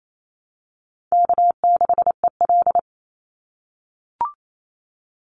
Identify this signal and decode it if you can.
Morse code, heard as a keyed tone, about 28 wpm, 700 Hz. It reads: K6EL
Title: NA SOTA Chaser Callsigns 28wpm